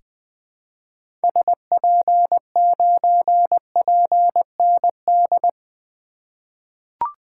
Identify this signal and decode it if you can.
Morse code, heard as a keyed tone, about 20 wpm, 700 Hz. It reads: SP9PND
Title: 2023 SOTA Activator Call Signs - 20wpm